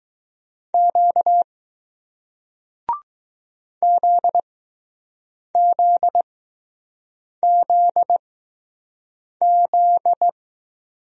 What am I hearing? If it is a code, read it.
Morse code, heard as a keyed tone, about 20 wpm, spaced as for 6 wpm, 700 Hz. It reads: Q ZZZZ